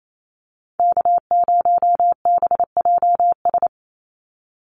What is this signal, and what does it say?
Morse code, heard as a keyed tone, about 28 wpm, 700 Hz. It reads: K0BJS
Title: USA Call Signs 28wpm